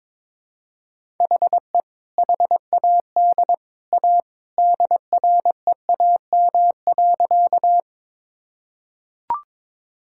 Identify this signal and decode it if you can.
Morse code, heard as a keyed tone, about 22 wpm, 700 Hz. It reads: HE HAD A DREAM.
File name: practice / 22wpm / Sentences with 2-Word Phrases - 22wpm